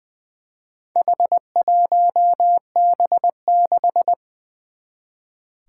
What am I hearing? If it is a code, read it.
Morse code, heard as a keyed tone, about 20 wpm, 700 Hz. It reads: H1B6
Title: Four Challenging Letter-Number Combinations 20wpm